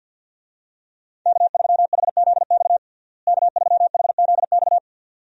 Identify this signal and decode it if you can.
Morse code, heard as a keyed tone, about 53 wpm, 700 Hz. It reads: K3HZX K3HZX
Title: USA Call Signs - 5 Characters - Repeated 2x - 50wpm